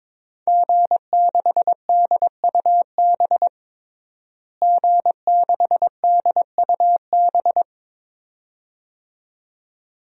Morse code, heard as a keyed tone, about 22 wpm, 700 Hz. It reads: G6DUB G6DUB
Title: DXCC-100 Call Signs - 5 Characters - Repeated 2x - 22wpm